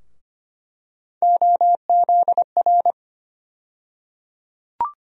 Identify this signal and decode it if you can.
Morse code, heard as a keyed tone, about 25 wpm, 700 Hz. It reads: OZR
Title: USA Call Sign Postfixes - 3 Letters 25wpm